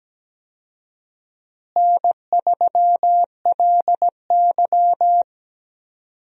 Morse code, heard as a keyed tone, about 17 wpm, 700 Hz. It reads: N3LY
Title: USA Call Signs - 4 Characters - Repeated 2x - 17wpm